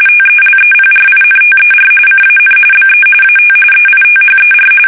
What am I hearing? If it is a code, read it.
RTTY, about 45.45 baud. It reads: TELEPRINTER CAN BE AT EITHER
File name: fsk850 / teletype-RTTY